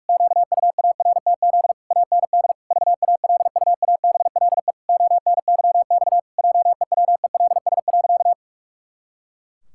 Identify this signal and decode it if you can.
Morse code, heard as a keyed tone, about 46 wpm, 700 Hz. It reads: QUARTZ AND VALUABLE ONYX JEWELS.
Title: Pangram Challenge v2 - 45wpm